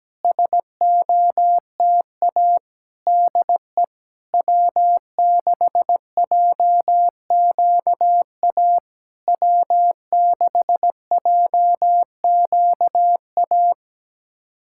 Morse code, heard as a keyed tone, about 17 wpm, 700 Hz. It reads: SOTA DE W6JQA W6JQA
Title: SOTA QSO Practice 17wpm